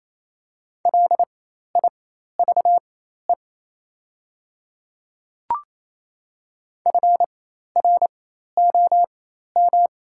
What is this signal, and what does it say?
Morse code, heard as a keyed tone, about 27 wpm, spaced as for 13 wpm, 700 Hz. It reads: LIVE FROM
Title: 4x Character Spacing - Top 200 Words - 28wpm